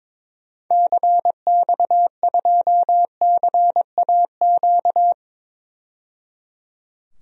Morse code, heard as a keyed tone, about 22 wpm, 700 Hz. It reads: CX2CAQ
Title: DXCC-100 Call Signs - 6 Characters - 22wpm